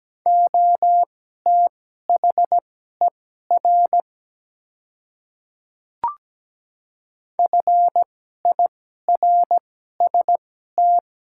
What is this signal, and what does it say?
Morse code, heard as a keyed tone, about 17 wpm, spaced as for 12 wpm, 700 Hz. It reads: OTHER FIRST
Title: 2x Character Spacing - Top 100 Words - 17wpm